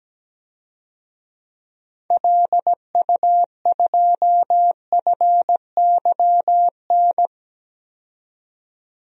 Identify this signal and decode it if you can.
Morse code, heard as a keyed tone, about 17 wpm, 700 Hz. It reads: LU2FYN
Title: DXCC-100 Call Signs - 6 Characters - 17wpm